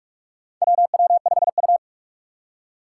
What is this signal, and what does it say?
Morse code, heard as a keyed tone, about 46 wpm, 700 Hz. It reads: WW5U
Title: USA Call Signs - 4 Characters - 6x Character Spacing - 45wpm